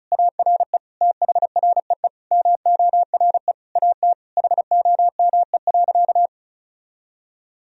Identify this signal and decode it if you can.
Morse code, heard as a keyed tone, about 36 wpm, 700 Hz. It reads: ARE THREE MORE AT HOME.
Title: Sentences from Top 1000 Words - Review Part 1 - 35wpm